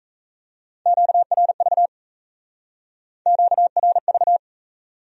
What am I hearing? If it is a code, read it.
Morse code, heard as a keyed tone, about 34 wpm, 700 Hz. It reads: QRV QRV
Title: Speed-Racing Q-Codes - 28wpm